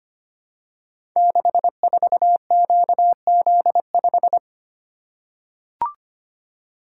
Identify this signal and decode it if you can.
Morse code, heard as a keyed tone, about 25 wpm, 700 Hz. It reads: 64QZ5
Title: Code Groups - 5 Letter:Numbers - 25wpm